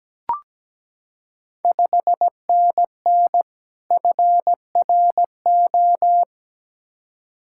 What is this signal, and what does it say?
Morse code, heard as a keyed tone, about 17 wpm, 690 Hz. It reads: 5NN FRO
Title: Ontario QSO Party Exchange 17wpm